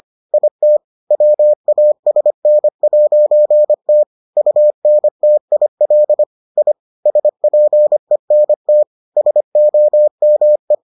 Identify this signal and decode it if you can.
Morse code, heard as a keyed tone, about 25 wpm, 600 Hz. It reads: IT WASN'T UNTIL I SPENT SOME